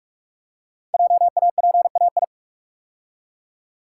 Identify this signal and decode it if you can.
Morse code, heard as a keyed tone, about 44 wpm, 700 Hz. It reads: JAPAI
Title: Code Groups - 5 Letter:Numbers - 45wpm